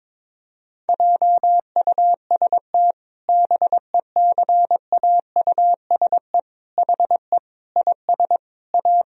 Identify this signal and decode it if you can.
Morse code, heard as a keyed tone, about 22 wpm, 700 Hz. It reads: JUST BECAUSE HE IS A
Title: Sentences from Top 1000 Words - Review Part 3 - 22wpm